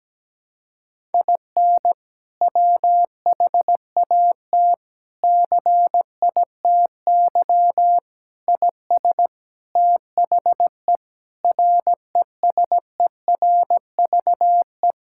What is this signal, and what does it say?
Morse code, heard as a keyed tone, about 17 wpm, 700 Hz. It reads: IN WHAT CITY IS THE RESERVE
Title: Sentences from Top 1000 Words - Review Part 6 - 17wpm